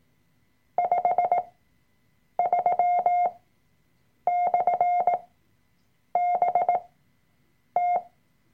Morse code, heard as a keyed tone, about 18 wpm, spaced as for 7 wpm, 710 Hz. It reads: 53/BT
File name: Workshop Dateien / CW-Online-Kurs / Lektion 003